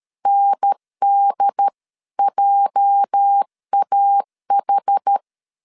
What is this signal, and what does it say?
Morse code, heard as a keyed tone, about 13 wpm, 800 Hz. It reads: NDJAH
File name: CW / DET / 1 9 1500C S C 12wpm 3811